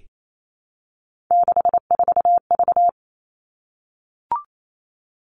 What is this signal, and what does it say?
Morse code, heard as a keyed tone, about 28 wpm, 700 Hz. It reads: 64V
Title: All Cut Numbers - 3-Digit 28wpm